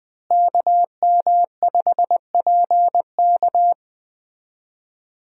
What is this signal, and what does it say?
Morse code, heard as a keyed tone, about 20 wpm, 700 Hz. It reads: KM5PK
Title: USA Call Signs - 5 Characters - Repeated 3x - 20wpm